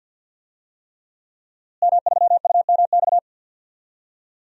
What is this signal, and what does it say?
Morse code, heard as a keyed tone, about 52 wpm, 700 Hz. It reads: M3UMX
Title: DXCC-100 Call Signs - 5 Characters - Repeated 3x - 50wpm